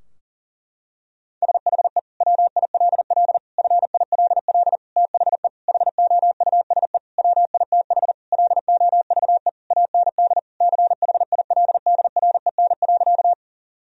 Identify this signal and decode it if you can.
Morse code, heard as a keyed tone, about 40 wpm, 700 Hz. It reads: SHE WILL FILL THE HOUSE WITH LOVE AND CHILDREN.
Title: Sentences from Top 1000 Words - Review Part 3 - 40wpm